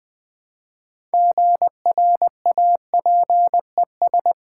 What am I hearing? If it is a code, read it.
Morse code, heard as a keyed tone, about 20 wpm, 700 Hz. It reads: GRAPES
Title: Easy Words with Hints - 20wpm